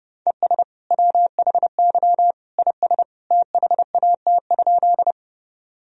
Morse code, heard as a keyed tone, about 30 wpm, 700 Hz. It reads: ES WHY IS THAT?
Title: Sentences from Top 200 Words and Abbreviations 30wpm